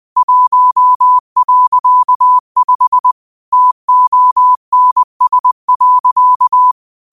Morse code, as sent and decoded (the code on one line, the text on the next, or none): .---- .-.-.- ..... / - --- -. ... .-.-.-
1.5 TONS.